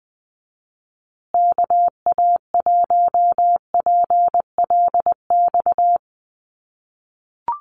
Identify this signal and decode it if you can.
Morse code, heard as a keyed tone, about 20 wpm, 700 Hz. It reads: KA1PLX
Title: USA Call Signs Repeated 3x 20wpm